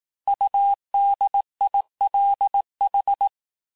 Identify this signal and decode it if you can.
Morse code, heard as a keyed tone, about 18 wpm, 800 Hz. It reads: UDILH
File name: cours-du-lundi files / cours-1594--teste-chiffre-vitesse-17mpm